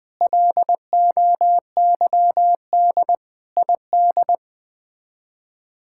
Morse code, heard as a keyed tone, about 20 wpm, 700 Hz. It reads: LOYD ID
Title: K1USN SST QSO Practice - 20wpm